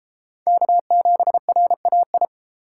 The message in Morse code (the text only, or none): K7RAI